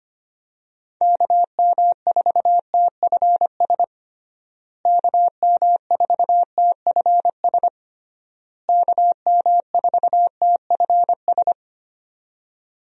Morse code, heard as a keyed tone, about 25 wpm, 700 Hz. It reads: KM4TFS KM4TFS KM4TFS
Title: USA Call Signs - 6 Characters - Repeated 3x - 25wpm